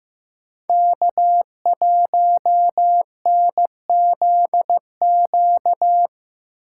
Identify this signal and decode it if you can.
Morse code, heard as a keyed tone, about 15 wpm, 700 Hz. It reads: K1NZQ
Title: USA Call Signs - 5 Characters - 8x Character Spacing - 15wpm